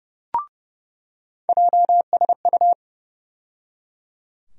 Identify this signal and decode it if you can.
Morse code, heard as a keyed tone, about 30 wpm, 700 Hz. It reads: JSU
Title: USA Call Sign Postfixes - 3 Letters 30wpm